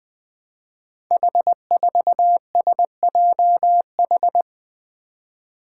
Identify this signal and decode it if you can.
Morse code, heard as a keyed tone, about 20 wpm, 700 Hz. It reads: H4SJH
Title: Five Challenging Letter-Number Combinations 20wpm